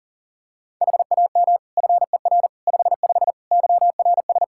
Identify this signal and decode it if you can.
Morse code, heard as a keyed tone, about 40 wpm, 700 Hz. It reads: HAM FER 55 YRS
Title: K1USN SST QSO Practice 40wpm